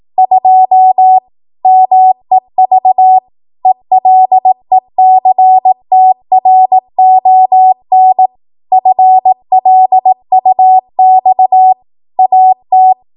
Morse code, heard as a keyed tone, about 18 wpm, 750 Hz. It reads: 2 MEV ELECTRON FLUX AT
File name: Morse / 18WPM